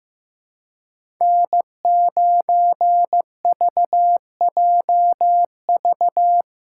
Morse code, heard as a keyed tone, about 15 wpm, 700 Hz. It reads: N9VJV
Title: USA Call Signs - 5 Characters - 4x Character Spacing - 15wpm